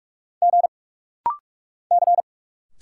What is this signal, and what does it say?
Morse code, heard as a keyed tone, about 46 wpm, 700 Hz. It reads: G C
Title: Single Letters - Rapid-Fire 45wpm